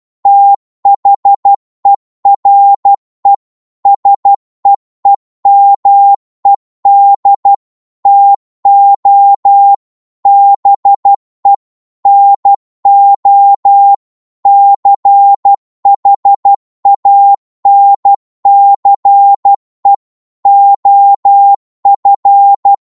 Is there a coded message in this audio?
Yes: Morse code from 0.3 to 22.7 s, heard as a keyed tone, about 12 wpm, 800 Hz. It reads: THERE SEEMED TO BE NO CHANCE OF